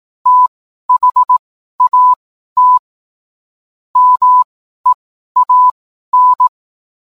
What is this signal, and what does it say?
Morse code, heard as a keyed tone, about 18 wpm, spaced as for 12 wpm, 1000 Hz. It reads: THAT MEAN